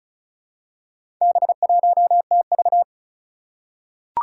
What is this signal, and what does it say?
Morse code, heard as a keyed tone, about 36 wpm, 690 Hz. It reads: B1TV
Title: Four Challenging Letter-Number Combinations 35wpm